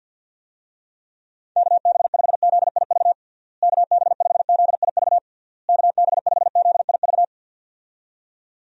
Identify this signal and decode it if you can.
Morse code, heard as a keyed tone, about 50 wpm, 700 Hz. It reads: KB5ZIV KB5ZIV KB5ZIV
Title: USA Call Signs - 6 Characters - Repeated 3x - 50wpm